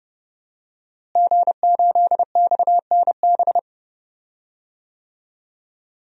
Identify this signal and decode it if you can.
Morse code, heard as a keyed tone, about 31 wpm, 700 Hz. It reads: G8XNB
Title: DXCC-100 Call Signs - 5 Characters - Repeated 2x - 30wpm